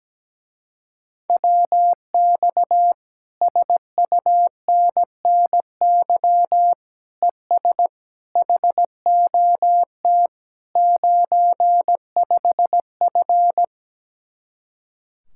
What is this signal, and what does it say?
Morse code, heard as a keyed tone, about 17 wpm, 700 Hz. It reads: WX SUNNY ES HOT 95F
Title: Speed-Racing Common QSO Phrases Encore - 17wpm